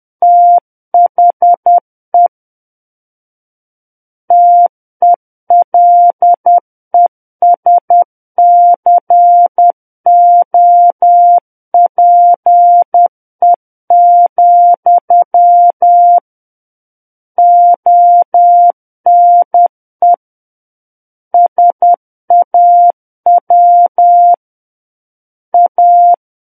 Morse code, heard as a keyed tone, about 10 wpm, 700 Hz. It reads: THE TELESCOPE, ONE SAW A
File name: Book One / 01-Chapter 1 - 10 WPM